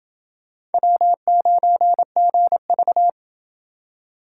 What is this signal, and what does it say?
Morse code, heard as a keyed tone, about 28 wpm, 700 Hz. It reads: W9GV